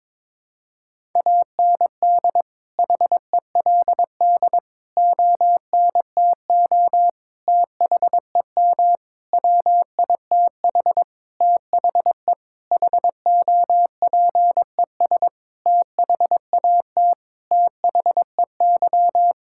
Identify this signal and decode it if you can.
Morse code, heard as a keyed tone, about 21 wpm, 700 Hz. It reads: AND HELD ONTO THEM WITH THE HOPES THAT THEY
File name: Sets of 9 Words - 22wpm